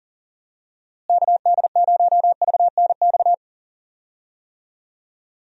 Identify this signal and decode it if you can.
Morse code, heard as a keyed tone, about 40 wpm, 700 Hz. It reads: KD0VNX
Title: USA Call Signs - 6 Characters - Repeated 2x - 40wpm